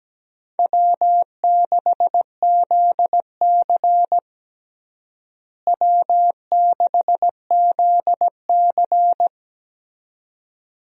Morse code, heard as a keyed tone, about 17 wpm, 700 Hz. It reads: W6ZC W6ZC
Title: USA Call Signs - 4 Characters - Repeated 2x - 17wpm